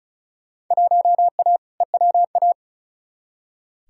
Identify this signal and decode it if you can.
Morse code, heard as a keyed tone, about 36 wpm, 690 Hz. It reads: 1A EWA
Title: Field Day Exchanges Encore - 35wpm